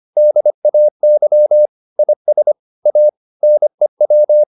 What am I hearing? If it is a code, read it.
Morse code, heard as a keyed tone, about 25 wpm, 600 Hz. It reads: DAY IS A NEW
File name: the old man and the sea part 1 25wpm